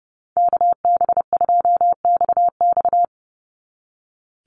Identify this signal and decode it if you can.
Morse code, heard as a keyed tone, about 29 wpm, 700 Hz. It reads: KB2XX
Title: NA SOTA Chaser Callsigns 30wpm